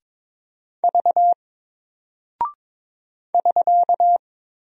Morse code, heard as a keyed tone, about 22 wpm, 700 Hz. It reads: V <SK>
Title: Single Character (Letter:Number:Punctuation:Prosign) Alt 22wpm